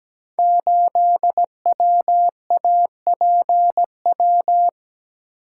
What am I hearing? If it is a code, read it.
Morse code, heard as a keyed tone, about 17 wpm, 700 Hz. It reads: 8WAPW
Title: Code Groups - 5 Letter:Numbers - 17wpm